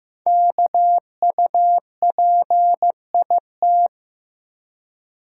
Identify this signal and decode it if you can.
Morse code, heard as a keyed tone, about 15 wpm, 700 Hz. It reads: KUPIT
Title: Code Groups - 5 Letter:Numbers - 15wpm